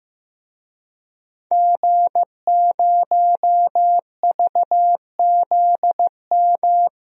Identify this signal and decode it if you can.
Morse code, heard as a keyed tone, about 15 wpm, 700 Hz. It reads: G0VZM
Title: DXCC-100 Call Signs - 5 Characters - Repeated 3x - 15wpm